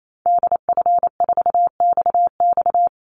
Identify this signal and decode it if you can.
Morse code, heard as a keyed tone, about 28 wpm, 700 Hz. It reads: DF4XX
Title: Call Signs Repeated 3x - 28wpm